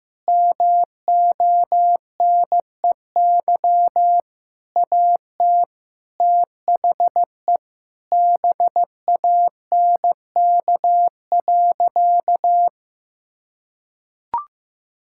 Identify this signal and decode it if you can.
Morse code, heard as a keyed tone, about 16 wpm, 700 Hz. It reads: MONEY AT THE BANK.
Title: Sentences from Top 1000 Words - Review Part 6 - 15wpm